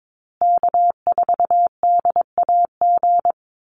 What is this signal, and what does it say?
Morse code, heard as a keyed tone, about 22 wpm, 700 Hz. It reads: K4DAG
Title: USA Call Signs Repeated 3x 22wpm